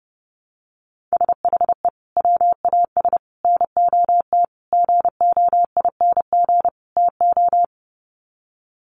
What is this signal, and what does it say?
Morse code, heard as a keyed tone, about 30 wpm, 700 Hz. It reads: SHE WAS NOT GOING TO
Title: Sets of 5 Words Encore2 30wpm